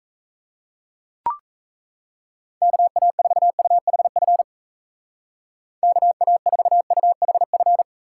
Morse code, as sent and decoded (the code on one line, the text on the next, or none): -.- .- ....- ..- .... ..-. / -.- .- ....- ..- .... ..-.
KA4UHF KA4UHF